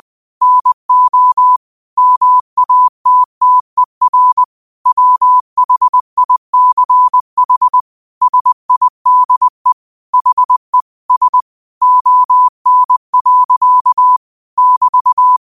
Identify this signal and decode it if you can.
Morse code, heard as a keyed tone, about 20 wpm, 1000 Hz. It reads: NO MATTER WHICH SIDE HE S ON. <BT>
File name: qotdfiles / 20wpm